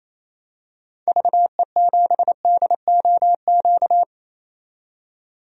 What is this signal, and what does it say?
Morse code, heard as a keyed tone, about 28 wpm, 700 Hz. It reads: VE7DOQ
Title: DXCC-100 Call Signs - 6 Characters - 28wpm